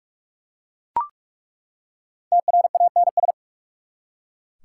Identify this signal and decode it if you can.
Morse code, heard as a keyed tone, about 44 wpm, 690 Hz. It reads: TRANS
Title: Prefixes and Suffixes 45wpm